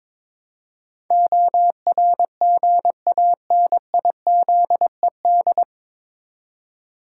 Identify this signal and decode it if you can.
Morse code, heard as a keyed tone, about 21 wpm, 700 Hz. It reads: ORGANIZED
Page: Morse Code Ninja practice 22wpm